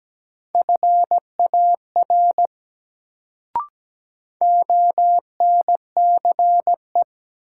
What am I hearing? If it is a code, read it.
Morse code, heard as a keyed tone, about 17 wpm, 700 Hz. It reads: FAR ONCE
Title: Sentences from Top 300 Words - New Word Drill - 17wpm